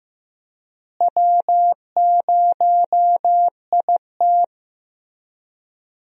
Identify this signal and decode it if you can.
Morse code, heard as a keyed tone, about 16 wpm, 700 Hz. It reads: W0IT
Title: USA Call Signs - 4 Characters - Repeated 2x - 15wpm